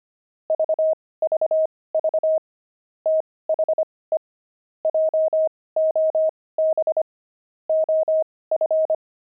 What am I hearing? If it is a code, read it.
Morse code, heard as a keyed tone, about 25 wpm, spaced as for 18 wpm, 630 Hz. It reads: VVV THE JOB OF
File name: pangram-fs-18-cs-25-peeves